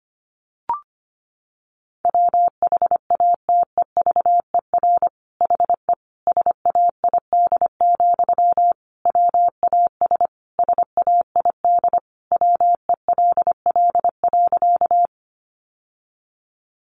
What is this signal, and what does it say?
Morse code, heard as a keyed tone, about 25 wpm, 700 Hz. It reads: WHATEVER HE SAID, WAS SAID WELL.
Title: Sentences from Top 1100 Words 25wpm